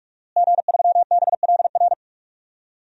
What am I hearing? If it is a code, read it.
Morse code, heard as a keyed tone, about 44 wpm, 700 Hz. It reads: G3BLR
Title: DXCC-100 Call Signs - 5 Characters - Repeated 3x - 45wpm